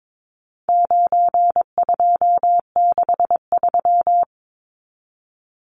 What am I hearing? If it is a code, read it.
Morse code, heard as a keyed tone, about 22 wpm, 700 Hz. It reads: 9263
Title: Numbers - 4-Digit 22wpm